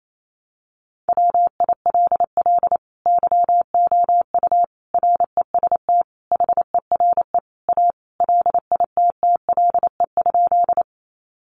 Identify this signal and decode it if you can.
Morse code, heard as a keyed tone, about 28 wpm, 700 Hz. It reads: WILL YOU REST HERE A LITTLE?
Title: Sentences from Top 500 Words 28wpm